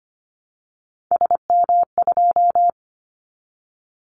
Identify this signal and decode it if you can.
Morse code, heard as a keyed tone, about 25 wpm, 700 Hz. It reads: SM2
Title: Callsign Prefixes for Top 25 DXCC - 25wpm